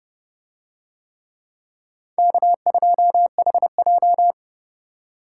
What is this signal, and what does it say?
Morse code, heard as a keyed tone, about 29 wpm, 700 Hz. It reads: K2HJ